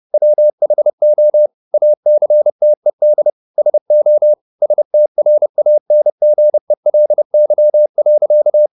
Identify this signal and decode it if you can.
Morse code, heard as a keyed tone, about 30 wpm, 600 Hz. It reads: WHO ACTED SO STRANGELY.
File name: the old man and the sea / the old man and the sea 1 30wpm